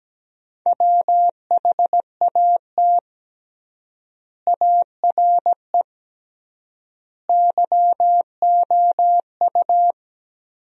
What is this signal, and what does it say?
Morse code, heard as a keyed tone, about 17 wpm, 700 Hz. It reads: WHAT ARE YOU
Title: 3x Word-Spacing - Sentences from Top 100 Words - 17wpm